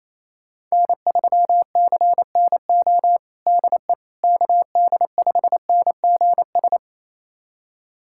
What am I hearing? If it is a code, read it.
Morse code, heard as a keyed tone, about 28 wpm, 700 Hz. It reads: N3CNO DE KD5NGS
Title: USA Callsign DE Callsign 28wpm